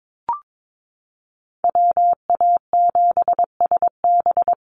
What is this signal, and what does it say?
Morse code, heard as a keyed tone, about 22 wpm, 700 Hz. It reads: WA7SB
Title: NA SOTA Chaser Callsigns 22wpm